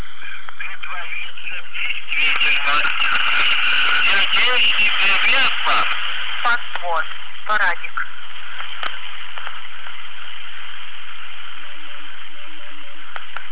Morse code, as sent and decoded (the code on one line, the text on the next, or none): none